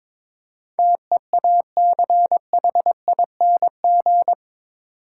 Morse code, heard as a keyed tone, about 22 wpm, 700 Hz. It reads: TEACHING
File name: practice / 22wpm / Eight Letter Words - 22wpm